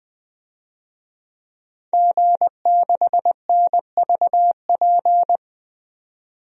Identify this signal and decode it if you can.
Morse code, heard as a keyed tone, about 20 wpm, 700 Hz. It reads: G6NVP
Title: DXCC-100 Call Signs - 5 Characters - 20wpm